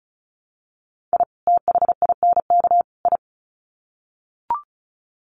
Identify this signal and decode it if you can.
Morse code, heard as a keyed tone, about 34 wpm, 700 Hz. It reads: I THINK I
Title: Sets of 3 Words 35wpm